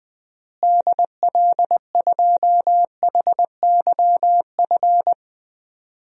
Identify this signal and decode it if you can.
Morse code, heard as a keyed tone, about 20 wpm, 700 Hz. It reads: DL2HYF
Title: DXCC-100 Call Signs - 6 Characters - Repeated 2x - 20wpm